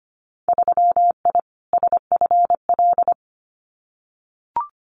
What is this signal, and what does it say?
Morse code, heard as a keyed tone, about 25 wpm, 700 Hz. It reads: 3I SFL
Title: Winter Field Day Exchanges 25wpm